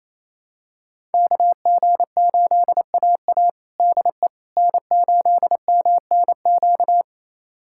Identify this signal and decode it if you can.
Morse code, heard as a keyed tone, about 28 wpm, 700 Hz. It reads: KG8AA DE N8MNQ
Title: USA Callsign DE Callsign - 5 Characters - 28wpm